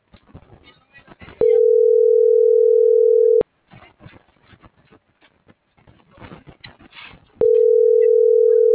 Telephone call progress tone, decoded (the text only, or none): ringback tone